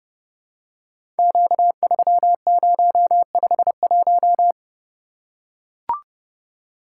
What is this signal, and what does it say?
Morse code, heard as a keyed tone, about 30 wpm, 700 Hz. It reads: Q3051
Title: Code Groups - 5 Letter:Numbers - 30wpm